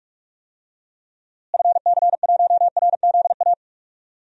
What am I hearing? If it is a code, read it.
Morse code, heard as a keyed tone, about 41 wpm, 700 Hz. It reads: FC1RZA
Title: DXCC-100 Call Signs - 6 Characters - Repeated 2x - 45wpm